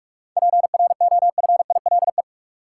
Morse code, heard as a keyed tone, about 42 wpm, 700 Hz. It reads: PROFILE
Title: Seven Letter Words - 45wpm